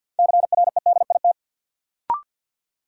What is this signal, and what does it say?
Morse code, heard as a keyed tone, about 53 wpm, 700 Hz. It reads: CREDIT
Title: Sentences from Top 1000 Words - New Word Drill - 50wpm